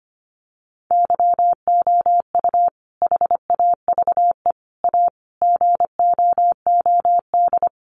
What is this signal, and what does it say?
Morse code, heard as a keyed tone, about 25 wpm, 700 Hz. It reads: YOU HAVE A GOOD
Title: Sentences from Top 1300 Words 25wpm